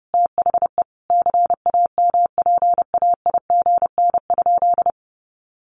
Morse code, heard as a keyed tone, about 30 wpm, 700 Hz. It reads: THE CAMPAIGN?
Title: Sentences from Top 1200 Words 30wpm